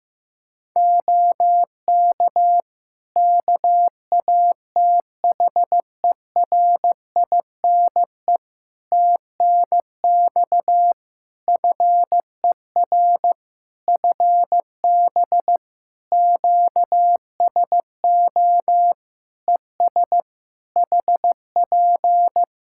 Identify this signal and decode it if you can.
Morse code, heard as a keyed tone, about 15 wpm, 700 Hz. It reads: OK KATHERINE TNX FER FB QSO ES HP
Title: K1USN SST QSO Practice 15wpm